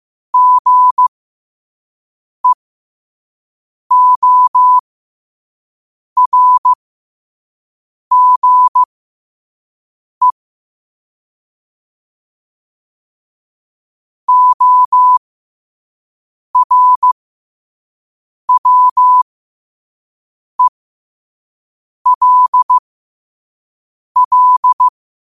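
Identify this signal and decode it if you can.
Morse code, heard as a keyed tone, about 15 wpm, spaced as for 5 wpm, 1000 Hz. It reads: GEORGE ORWELL
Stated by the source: Morse code practice at 5 wpm.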